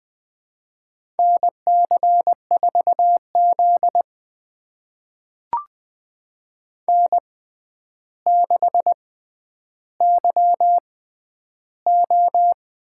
Morse code, heard as a keyed tone, about 20 wpm, spaced as for 10 wpm, 700 Hz. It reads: NC4Z N6YO
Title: USA Call Signs - 4 Characters - 6x Character Spacing - 20wpm